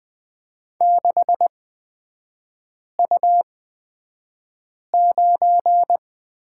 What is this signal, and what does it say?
Morse code, heard as a keyed tone, about 20 wpm, spaced as for 5 wpm, 690 Hz. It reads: 6U9
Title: 229 Lesson 37e - Letter x - Three Characters 20wpm